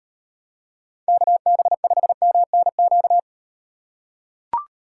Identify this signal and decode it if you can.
Morse code, heard as a keyed tone, about 38 wpm, 700 Hz. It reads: KB5MNQ